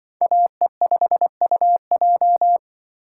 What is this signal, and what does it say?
Morse code, heard as a keyed tone, about 24 wpm, 700 Hz. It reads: AE5UJ
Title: Speed Racing - USA 2x2 Call Signs 20wpm